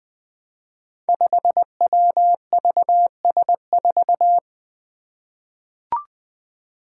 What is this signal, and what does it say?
Morse code, heard as a keyed tone, about 20 wpm, 700 Hz. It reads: 5WVS4
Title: Five Challenging Letter-Number Combinations 20wpm